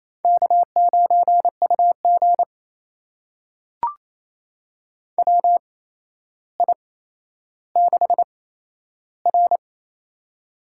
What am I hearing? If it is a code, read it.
Morse code, heard as a keyed tone, about 28 wpm, spaced as for 12 wpm, 700 Hz. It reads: K9UG WI6R
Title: USA Call Signs - 4 Characters - 8x Character Spacing - 28wpm